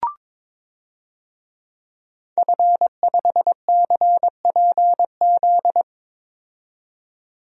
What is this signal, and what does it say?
Morse code, heard as a keyed tone, about 22 wpm, 700 Hz. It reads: F5CPZ
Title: DXCC-100 Call Signs - 5 Characters - 22wpm